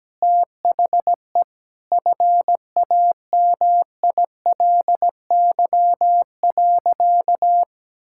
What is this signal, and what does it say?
Morse code, heard as a keyed tone, about 17 wpm, 700 Hz. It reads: THE FAMILY.
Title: Sentences from Top 1000 Words - Review Part 5 - 17wpm